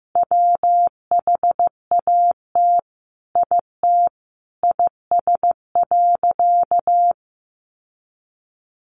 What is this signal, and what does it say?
Morse code, heard as a keyed tone, about 15 wpm, 700 Hz. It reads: WHAT IT IS.
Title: Sentences from Top 700 Words 15wpm